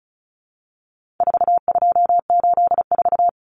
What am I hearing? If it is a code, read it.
Morse code, heard as a keyed tone, about 34 wpm, 700 Hz. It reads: 4284
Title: Common Cut Numbers - 4-Digit 35wpm